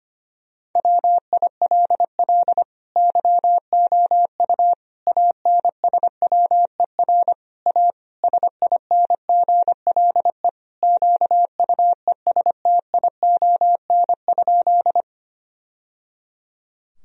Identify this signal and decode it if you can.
Morse code, heard as a keyed tone, about 25 wpm, 700 Hz. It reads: WILL YOU ANSWER A SINGLE QUESTION?
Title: Sentences from Top 1000 Words - Review Part 4 - 25wpm